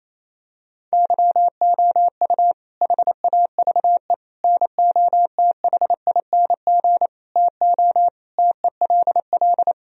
Morse code, heard as a keyed tone, about 28 wpm, 700 Hz. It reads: YOU HAVE NOTHING TO TELL
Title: Sentences from Top 1000 Words - Review Part 4 - 28wpm